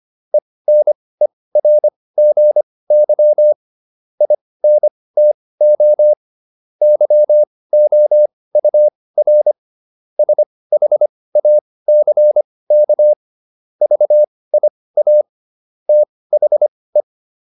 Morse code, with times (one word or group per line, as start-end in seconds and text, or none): 0.3-3.5 ENERGY
4.2-6.1 INTO
6.8-9.5 YOUR
10.2-13.1 SHACK
13.8-15.2 VIA
15.9-17.0 THE